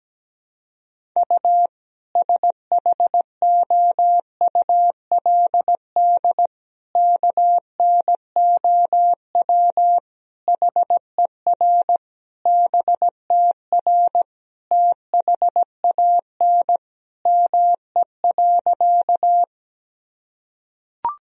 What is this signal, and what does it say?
Morse code, heard as a keyed tone, about 17 wpm, 700 Hz. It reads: U SHOULD KNOW HER BTR THAN ME.
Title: Sentences from Top 400 Words and Abbreviations 17wpm